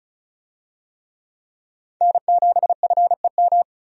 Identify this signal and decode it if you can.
Morse code, heard as a keyed tone, about 34 wpm, 700 Hz. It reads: N7FEM
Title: USA Call Signs - 5 Characters - 35wpm